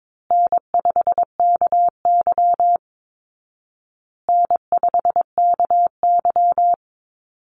Morse code, heard as a keyed tone, about 22 wpm, 700 Hz. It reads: N5KY N5KY
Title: Easy Call Signs Repeated 3x - 22wpm